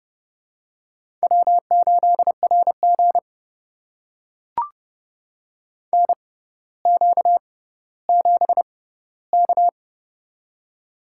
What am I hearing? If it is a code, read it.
Morse code, heard as a keyed tone, about 30 wpm, spaced as for 15 wpm, 700 Hz. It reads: W8RG NQ7K
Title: USA Call Signs - 6x Character Spacing - 30wpm